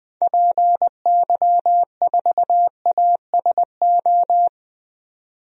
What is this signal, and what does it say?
Morse code, heard as a keyed tone, about 20 wpm, 700 Hz. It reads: PY4ASO
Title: DXCC-100 Call Signs - 6 Characters - 20wpm